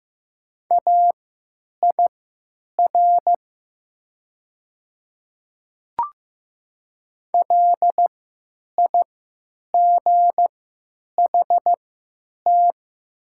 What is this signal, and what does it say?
Morse code, heard as a keyed tone, about 15 wpm, spaced as for 8 wpm, 700 Hz. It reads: AIR LIGHT
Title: 3x Character Spacing - Top 300 Words - 15wpm